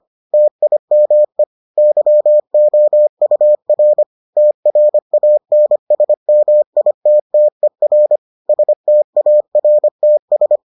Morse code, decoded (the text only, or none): TIME YOUR TRANSMITTER STARTS